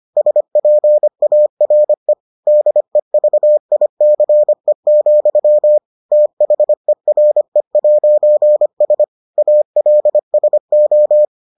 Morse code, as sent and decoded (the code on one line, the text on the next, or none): ... .--. .- .-. / -.. ...- .. -.-. --..-- / - .... .-. .----. ... / .- .-.. ... ---
SPARE DEVICE, THERE'S ALSO